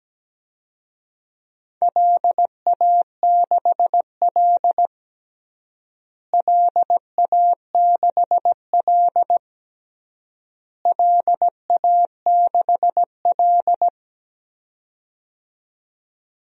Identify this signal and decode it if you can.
Morse code, heard as a keyed tone, about 17 wpm, 700 Hz. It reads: LA6L LA6L LA6L
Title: DXCC-100 Call Signs - 4 Characters - Repeated 3x - 17wpm